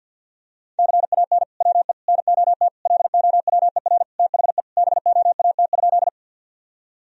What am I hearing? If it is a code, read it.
Morse code, heard as a keyed tone, about 52 wpm, 700 Hz. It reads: CAN WE NOT LOWER THE BOAT?
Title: Sentences from Top 1000 Words - Review Part 4 - 50wpm